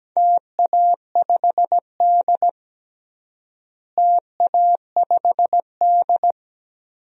Morse code, heard as a keyed tone, about 17 wpm, 700 Hz. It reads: TA5D TA5D
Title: DXCC-100 Call Signs - 4 Characters - Repeated 2x - 17wpm